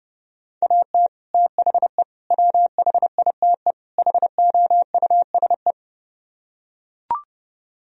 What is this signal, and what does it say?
Morse code, heard as a keyed tone, about 32 wpm, 700 Hz. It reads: AT THE WHITE HOUSE